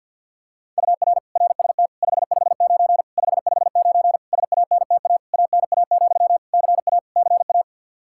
Morse code, heard as a keyed tone, about 52 wpm, 700 Hz. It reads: UR RST 559 559 SANTA ANA, CA CA
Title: Practice QSO - Exchange 1 - Beginning - 50wpm